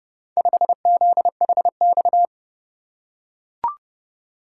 Morse code, heard as a keyed tone, about 30 wpm, 700 Hz. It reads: 5ZHX